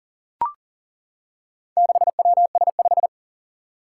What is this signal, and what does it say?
Morse code, heard as a keyed tone, about 43 wpm, 700 Hz. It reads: 6WS5
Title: Four Challenging Letter-Number Combinations 40wpm